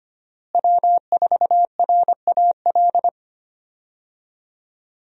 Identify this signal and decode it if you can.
Morse code, heard as a keyed tone, about 25 wpm, 700 Hz. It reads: W4RAL
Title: USA Call Signs - 5 Characters - Repeated 2x - 25wpm